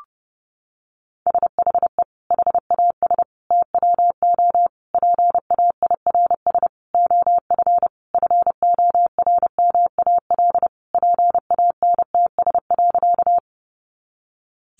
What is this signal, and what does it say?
Morse code, heard as a keyed tone, about 30 wpm, 700 Hz. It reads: SHE HAS TWO PAIRS OF FORMAL PANTS.
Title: Additional Easy Phrases 30wpm